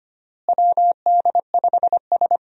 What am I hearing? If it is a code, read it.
Morse code, heard as a keyed tone, about 25 wpm, 700 Hz. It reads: WD5S
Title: USA Call Signs - 4 Characters - Repeated 2x - 25wpm